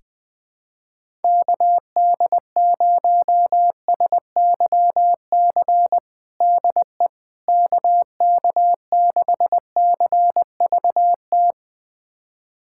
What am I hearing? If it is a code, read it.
Morse code, heard as a keyed tone, about 20 wpm, 690 Hz. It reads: KD0SYC DE KK6CVT
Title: USA Callsign DE Callsign 20wpm